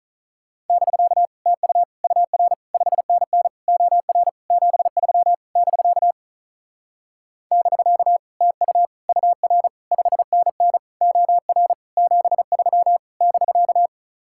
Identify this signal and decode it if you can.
Morse code, heard as a keyed tone, about 43 wpm, 700 Hz. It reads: <BK> TU UR 5NN OR 73 <BK> <BK> TU UR 5NN OR 73 <BK>
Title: Speed-Racing Common QSO Phrases - 35wpm